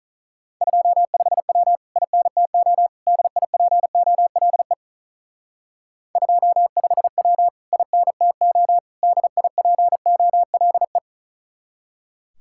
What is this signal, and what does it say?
Morse code, heard as a keyed tone, about 45 wpm, 700 Hz. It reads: 25W INTO DIPOLE 25W INTO DIPOLE